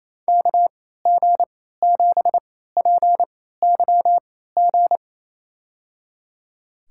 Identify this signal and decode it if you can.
Morse code, heard as a keyed tone, about 28 wpm, spaced as for 16 wpm, 700 Hz. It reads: KG7PYG
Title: USA Call Signs - 3x Character Spacing - 28wpm